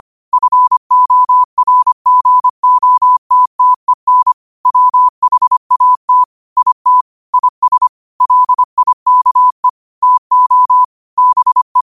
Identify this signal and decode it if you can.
Morse code, heard as a keyed tone, about 25 wpm, 1000 Hz. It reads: FORGOTTEN WHAT IT IS LIKE TO BE
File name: qotdfiles / 25wpm